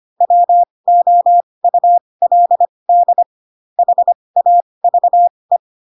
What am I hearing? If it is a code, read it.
Morse code, heard as a keyed tone, about 25 wpm, spaced as for 20 wpm, 700 Hz. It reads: WOULD HAVE